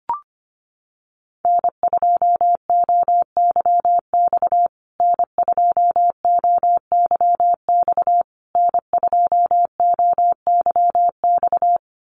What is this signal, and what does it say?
Morse code, heard as a keyed tone, about 25 wpm, 700 Hz. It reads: N2OYX N2OYX N2OYX
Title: USA Call Signs Repeated 3x 25wpm